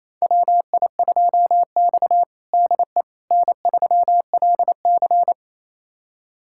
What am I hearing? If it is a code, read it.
Morse code, heard as a keyed tone, about 27 wpm, 700 Hz. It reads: WI2X DE N3LC
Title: USA Callsign DE Callsign - 4 Characters - 28wpm